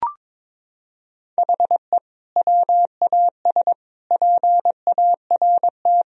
Morse code, heard as a keyed tone, about 22 wpm, 700 Hz. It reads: HE WAS PART
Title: Sentences from Top 1000 Words - Review Part 2 - 22wpm